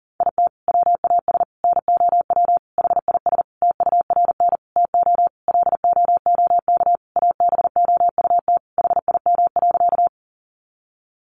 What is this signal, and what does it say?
Morse code, heard as a keyed tone, about 40 wpm, 700 Hz. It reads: IT WAS NOW HIS TURN TO LOOK ABOUT HIM.
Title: Sentences from Top 200 Words 40wpm